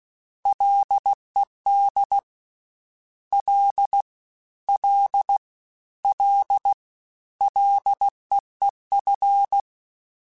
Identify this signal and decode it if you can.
Morse code, heard as a keyed tone, about 16 wpm, 800 Hz. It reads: LED L L L LEEF
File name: Ls / Single L m3s / Single L Prefixe 3x2 Twice @15